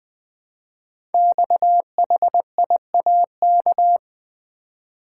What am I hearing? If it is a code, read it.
Morse code, heard as a keyed tone, about 20 wpm, 700 Hz. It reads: XHIAK
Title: Code Groups - 5 Letter:Numbers - 20wpm